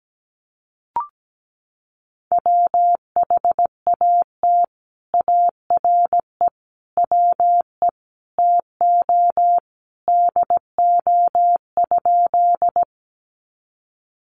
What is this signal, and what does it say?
Morse code, heard as a keyed tone, about 17 wpm, 700 Hz. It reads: WHAT ARE WE TO DO?
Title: Sentences from Top 100 Words 17wpm